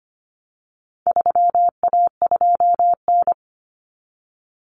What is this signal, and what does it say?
Morse code, heard as a keyed tone, about 25 wpm, 700 Hz. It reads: 3A2N
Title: Common Cut Numbers - 4-Digit 25wpm